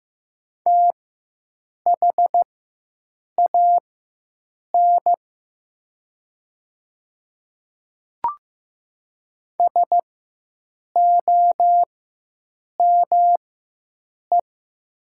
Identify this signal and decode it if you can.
Morse code, heard as a keyed tone, about 15 wpm, spaced as for 7 wpm, 700 Hz. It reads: THAN SOME
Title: 4x Character Spacing - Top 200 Words - 15wpm